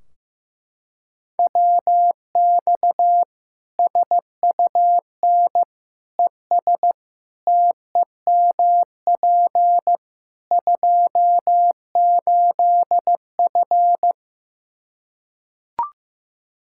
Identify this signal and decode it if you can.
Morse code, heard as a keyed tone, about 15 wpm, 700 Hz. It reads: WX SUN ES TEMP 28F
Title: Practice QSO - Exchange 2 - Info - 15wpm